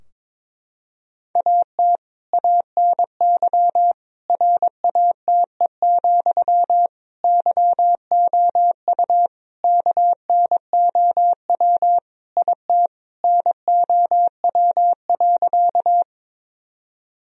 Morse code, heard as a keyed tone, about 22 wpm, 700 Hz. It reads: AT ANY RATE, YOU KNOW IT NOW.
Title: Sentences from Top 1000 Words - Review Part 4 - 22wpm